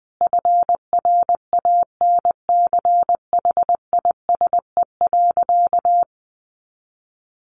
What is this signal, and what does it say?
Morse code, heard as a keyed tone, about 20 wpm, 700 Hz. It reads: FRANCHISE.
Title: A Sound Of Thunder 20wpm